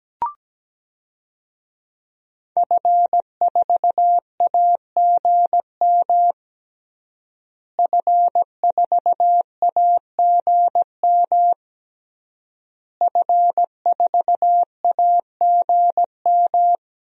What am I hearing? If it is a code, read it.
Morse code, heard as a keyed tone, about 17 wpm, 700 Hz. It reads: F4AGM F4AGM F4AGM
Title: DXCC-100 Call Signs - 5 Characters - Repeated 3x - 17wpm